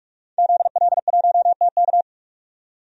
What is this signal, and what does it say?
Morse code, heard as a keyed tone, about 43 wpm, 700 Hz. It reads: ZL1TK
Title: DXCC-100 Call Signs - 5 Characters - 45wpm